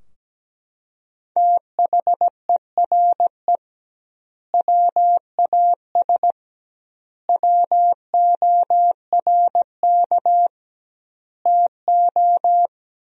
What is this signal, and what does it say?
Morse code, heard as a keyed tone, about 17 wpm, 700 Hz. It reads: THERE WAS WORK TO
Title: 2x Word-Spacing - Sentences from Top 200 Words - 17wpm